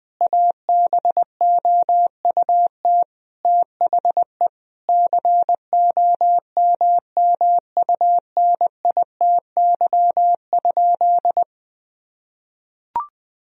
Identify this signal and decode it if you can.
Morse code, heard as a keyed tone, about 20 wpm, 700 Hz. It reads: ABOUT THE COMMUNITY?
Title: Sentences from Top 1000 Words - Review Part 4 - 20wpm